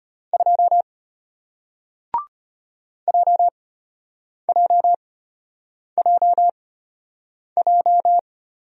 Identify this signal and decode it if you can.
Morse code, heard as a keyed tone, about 33 wpm, spaced as for 8 wpm, 700 Hz. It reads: 2 JJJJ